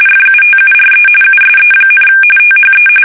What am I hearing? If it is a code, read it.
RTTY, about 45.45 baud. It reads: FUTURE USE, THE
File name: fsk850 / battleshipnj